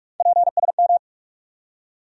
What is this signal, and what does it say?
Morse code, heard as a keyed tone, about 42 wpm, 700 Hz. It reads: PSM
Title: USA Call Sign Postfixes - 3 Letters 45wpm